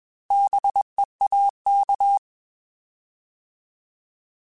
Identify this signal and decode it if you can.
Morse code, heard as a keyed tone, about 21 wpm, 800 Hz. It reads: BEAK
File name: Bs @20